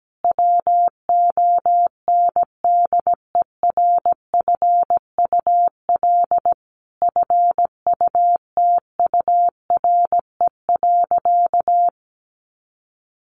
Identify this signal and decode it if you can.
Morse code, heard as a keyed tone, about 17 wpm, 700 Hz. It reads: WONDERFUL FUTURE.
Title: Sentences from Top 1100 Words 17wpm